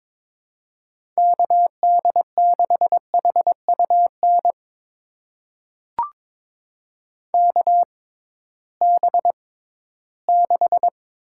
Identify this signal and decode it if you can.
Morse code, heard as a keyed tone, about 22 wpm, spaced as for 14 wpm, 700 Hz. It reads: KD6HUN KB6
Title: USA Call Signs - 6 Characters - 6x Character Spacing - 22wpm